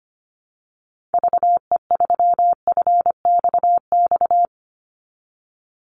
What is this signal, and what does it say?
Morse code, heard as a keyed tone, about 25 wpm, 700 Hz. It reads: VE3FXX
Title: Call Signs Repeated 3x - 25wpm